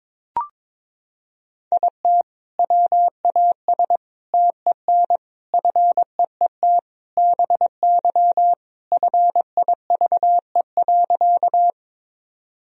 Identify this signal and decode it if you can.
Morse code, heard as a keyed tone, about 22 wpm, 700 Hz. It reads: IT WAS TEN FEET BY FIVE.
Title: Sentences from Top 1000 Words - Review Part 2 - 22wpm